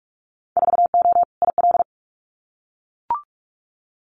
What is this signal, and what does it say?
Morse code, heard as a keyed tone, about 46 wpm, 700 Hz. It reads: 4O IL
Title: Winter Field Day Exchanges Encore 45wpm